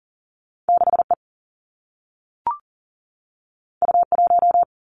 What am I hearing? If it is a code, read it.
Morse code, heard as a keyed tone, about 39 wpm, 700 Hz. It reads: 6E U1
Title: Two Challenging Letter-Number Combinations 40wpm